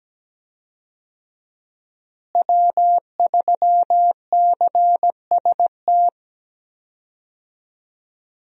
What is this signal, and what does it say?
Morse code, heard as a keyed tone, about 17 wpm, 700 Hz. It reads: W3CST
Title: USA Call Signs - 5 Characters - Repeated 2x - 17wpm